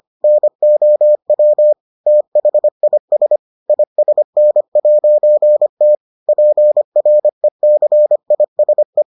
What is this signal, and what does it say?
Morse code, heard as a keyed tone, about 25 wpm, 600 Hz. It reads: NOW THIS ISN'T PRECISE